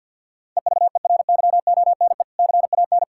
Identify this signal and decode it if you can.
Morse code, heard as a keyed tone, about 50 wpm, 700 Hz. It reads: EVERYONE CAN
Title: Sentences from Top 1600 Words 50wpm